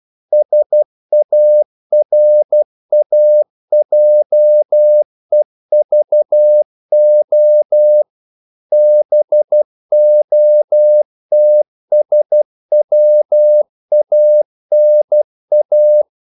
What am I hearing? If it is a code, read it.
Morse code, heard as a keyed tone, about 12 wpm, 600 Hz. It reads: SARAJEVO BOTSWANA
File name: CW / Practice / NationalCapitals12